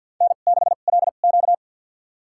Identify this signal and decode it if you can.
Morse code, heard as a keyed tone, about 53 wpm, spaced as for 37 wpm, 700 Hz. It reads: N6LQ
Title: USA Call Signs - 4 Characters - 2x Character Spacing - 50wpm